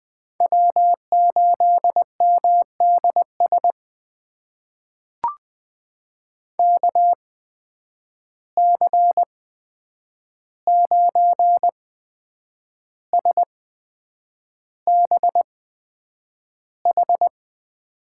Morse code, heard as a keyed tone, about 20 wpm, spaced as for 8 wpm, 700 Hz. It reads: W8MDS KC9SBH
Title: USA Call Signs - 8x Character Spacing - 20wpm